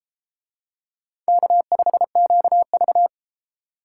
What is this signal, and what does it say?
Morse code, heard as a keyed tone, about 32 wpm, 700 Hz. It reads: K5QV